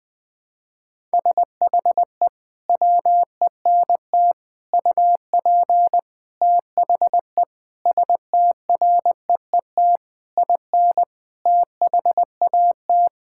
Morse code, heard as a keyed tone, about 20 wpm, 700 Hz. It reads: SHE WENT UP THE STREET IN THAT
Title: Sentences from Top 1000 Words - Review Part 5 - 20wpm